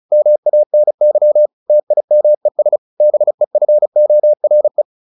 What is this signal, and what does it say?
Morse code, heard as a keyed tone, about 34 wpm, 600 Hz. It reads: MANY TIMES BEFORE
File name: the old man and the sea / the old man and the sea 1 35wpm